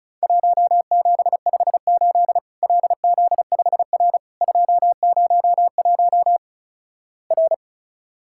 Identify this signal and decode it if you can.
Morse code, heard as a keyed tone, about 36 wpm, 700 Hz. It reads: 1758 LZ5R 201 R
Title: Worked All Europe DX Contest - QTC Exchange - 35wpm